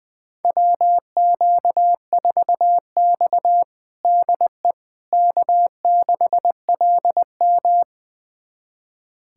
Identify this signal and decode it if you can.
Morse code, heard as a keyed tone, about 21 wpm, 700 Hz. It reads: WQ4X DE K6LM
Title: USA Callsign DE Callsign - 4 Characters - 20wpm